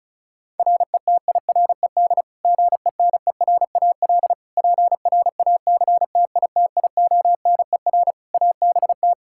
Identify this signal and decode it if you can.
Morse code, heard as a keyed tone, about 36 wpm, 700 Hz. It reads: RETIRED GENERAL PRACTITIONER ABT
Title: Practice QSO - Exchange 3 - More Info - 35wpm